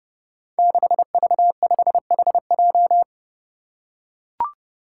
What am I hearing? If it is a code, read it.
Morse code, heard as a keyed tone, about 30 wpm, 700 Hz. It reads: 6V5HJ
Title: Five Challenging Letter-Number Combinations 30wpm